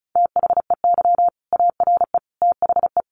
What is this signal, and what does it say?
Morse code, heard as a keyed tone, about 34 wpm, 700 Hz. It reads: THEY ARE THE
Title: Sentences from Top 1200 Words 35wpm